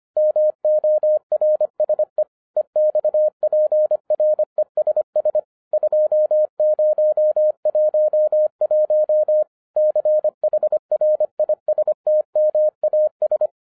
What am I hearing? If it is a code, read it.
Morse code, heard as a keyed tone, about 25 wpm, 610 Hz. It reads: MORSE EXPRESS 2011 CHRISTMAS
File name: 2011-10-mp3 / n14-MORSE EXPRESS 2011 CHRISTMAS KEY 25wpm